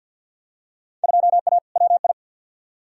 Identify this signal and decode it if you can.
Morse code, heard as a keyed tone, about 53 wpm, 700 Hz. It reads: 2A WI
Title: Field Day Exchanges Encore - 50wpm